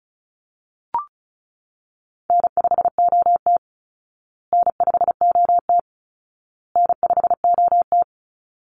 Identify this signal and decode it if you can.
Morse code, heard as a keyed tone, about 34 wpm, 700 Hz. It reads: N5OT N5OT N5OT
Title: Easy Call Signs Repeated 3x - 35wpm